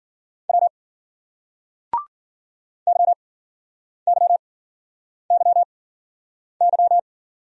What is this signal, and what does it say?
Morse code, heard as a keyed tone, about 50 wpm, spaced as for 9 wpm, 700 Hz. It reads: W YYYY
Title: Speed-Racing Single Letters 40wpm